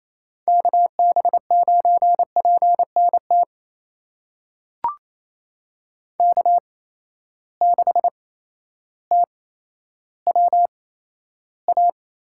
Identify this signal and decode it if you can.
Morse code, heard as a keyed tone, about 28 wpm, spaced as for 13 wpm, 700 Hz. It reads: KB9PNT K6TWA
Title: USA Call Signs - 8x Character Spacing - 28wpm